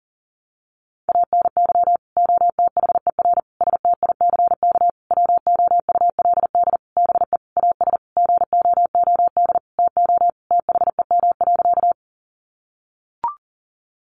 Morse code, heard as a keyed tone, about 40 wpm, 700 Hz. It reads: ANY OTHER STICK WOULD BE AS GOOD TO THEM.
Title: Sentences from Top 1200 Words 40wpm